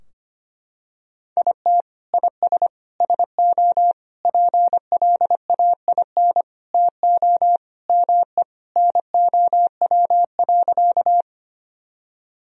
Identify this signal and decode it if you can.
Morse code, heard as a keyed tone, about 25 wpm, 700 Hz. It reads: IT IS SO PLAIN TO ME NOW.
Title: Sentences from Top 1000 Words - Review Part 2 - 25wpm